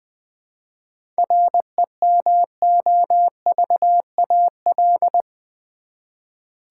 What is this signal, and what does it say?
Morse code, heard as a keyed tone, about 20 wpm, 700 Hz. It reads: REMOVAL
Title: 7 Letter Words - First 3 Letters Given - 20wpm